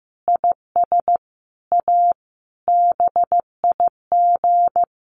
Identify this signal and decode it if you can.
Morse code, heard as a keyed tone, about 15 wpm, 700 Hz. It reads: IS A BIG
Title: Sentences from Top 1500 Words 15wpm